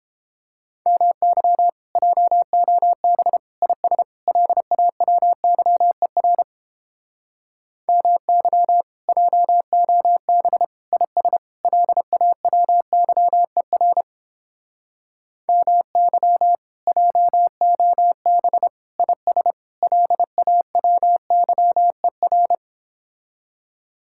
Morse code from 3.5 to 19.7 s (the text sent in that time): IS LAWYER MY JOB IS LAWYER MY JOB IS